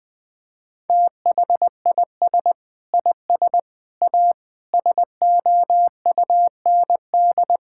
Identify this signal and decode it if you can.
Morse code, heard as a keyed tone, about 20 wpm, 700 Hz. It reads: THIS IS A SOUND
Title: Sentences from Top 1000 Words - Review Part 6 - 20wpm